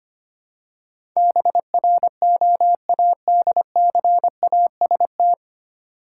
Morse code, heard as a keyed tone, about 25 wpm, 700 Hz. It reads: BROADCAST
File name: practice / 25wpm / 9 Letter Words - First 3 Letters Given - 25wpm